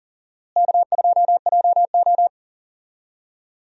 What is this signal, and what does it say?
Morse code, heard as a keyed tone, about 40 wpm, 700 Hz. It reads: K2JO
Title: USA Call Signs - 4 Characters - Repeated 2x - 40wpm